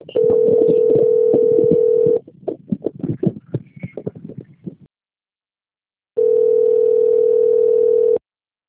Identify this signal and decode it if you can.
Ringback tone (North America), 440 + 480 Hz.